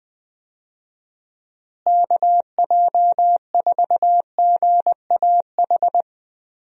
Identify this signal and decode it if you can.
Morse code, heard as a keyed tone, about 20 wpm, 700 Hz. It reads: KJ4GAH
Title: USA Call Signs - 6 Characters - Repeated 3x - 20wpm